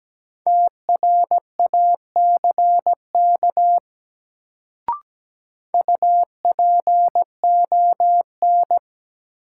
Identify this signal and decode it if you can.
Morse code, heard as a keyed tone, about 17 wpm, 700 Hz. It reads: TRACK UPON
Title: Sentences from Top 1100 Words - New Word Drill Alt - 17wpm